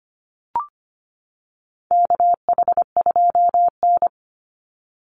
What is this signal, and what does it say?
Morse code, heard as a keyed tone, about 25 wpm, 700 Hz. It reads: KH2N
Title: Call Signs - 25wpm